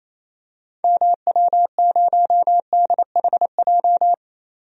Morse code, heard as a keyed tone, about 28 wpm, 700 Hz. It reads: MW0DHJ
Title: DXCC-100 Call Signs - 6 Characters - 28wpm